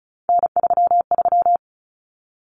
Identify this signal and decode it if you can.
Morse code, heard as a keyed tone, about 34 wpm, 700 Hz. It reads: N33
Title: Common Cut Numbers - 3-Digit 35wpm